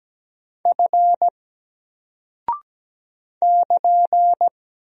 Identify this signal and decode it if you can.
Morse code, heard as a keyed tone, about 17 wpm, 700 Hz. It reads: F <KN>
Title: Single Character (Letter:Number:Punctuation:Prosign) 17wpm